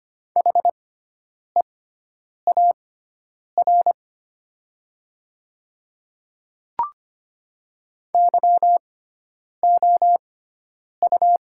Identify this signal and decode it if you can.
Morse code, heard as a keyed tone, about 25 wpm, spaced as for 9 wpm, 700 Hz. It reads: HEAR YOU
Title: 6x Character Spacing - Top 500 Words - 25wpm